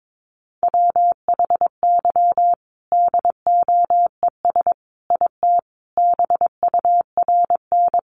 Morse code, heard as a keyed tone, about 22 wpm, 700 Hz. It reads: WHY DOES IT BURN
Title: Sentences from Top 1300 Words 22wpm